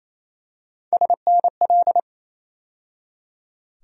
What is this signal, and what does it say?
Morse code, heard as a keyed tone, about 27 wpm, 700 Hz. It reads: SNL
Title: USA Call Sign Postfixes - 3 Letters 28wpm